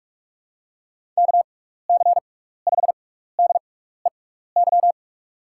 Morse code, heard as a keyed tone, about 46 wpm, spaced as for 16 wpm, 700 Hz. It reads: KC5DEY
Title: USA Call Signs - 6 Characters - 6x Character Spacing - 45wpm